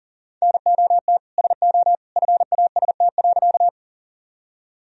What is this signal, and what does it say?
Morse code, heard as a keyed tone, about 40 wpm, 700 Hz. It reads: NOT SO FAST.
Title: Sentences from Top 1000 Words - Review Part 2 - 40wpm